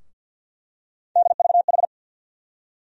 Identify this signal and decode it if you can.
Morse code, heard as a keyed tone, about 51 wpm, 700 Hz. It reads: DVH
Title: USA Call Sign Postfixes - 3 Letters 50wpm